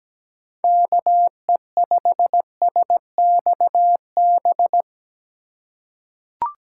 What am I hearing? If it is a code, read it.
Morse code, heard as a keyed tone, about 17 wpm, 700 Hz. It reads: KE5SXB
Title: USA Call Signs - 6 Characters - 4x Character Spacing - 17wpm